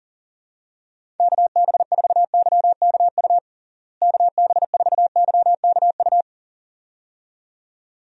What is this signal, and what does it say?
Morse code, heard as a keyed tone, about 40 wpm, 700 Hz. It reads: KB4YKU KB4YKU
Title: USA Call Signs - 6 Characters - Repeated 2x - 40wpm